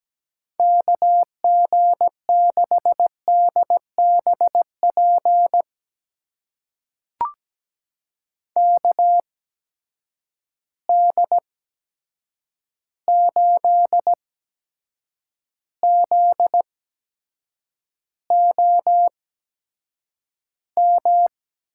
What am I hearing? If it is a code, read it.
Morse code, heard as a keyed tone, about 17 wpm, spaced as for 7 wpm, 700 Hz. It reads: KG6DBP KD8ZOM
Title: USA Call Signs - 6 Characters - 8x Character Spacing - 17wpm